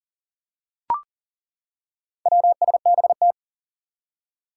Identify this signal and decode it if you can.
Morse code, heard as a keyed tone, about 37 wpm, 700 Hz. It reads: WSBT
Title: Four Challenging Letter-Number Combinations 40wpm